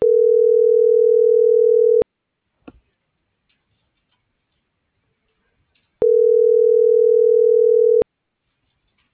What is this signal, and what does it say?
Ringback tone (North America), 440 + 480 Hz.